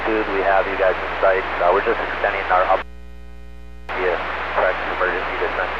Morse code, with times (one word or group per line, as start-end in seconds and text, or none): none